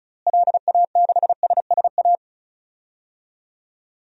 Morse code, heard as a keyed tone, about 36 wpm, 700 Hz. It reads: LA6SSA
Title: DXCC-100 Call Signs - 6 Characters - Repeated 2x - 35wpm